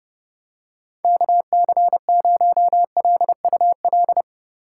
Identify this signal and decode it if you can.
Morse code, heard as a keyed tone, about 30 wpm, 700 Hz. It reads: KC0LUL
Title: USA Call Signs - 6 Characters - 3x Character Spacing - 30wpm